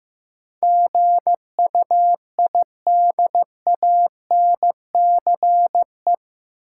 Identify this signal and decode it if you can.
Morse code, heard as a keyed tone, about 15 wpm, 700 Hz. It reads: GUIDANCE
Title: Sentences from Top 1600 Words - New Word Drill - 15wpm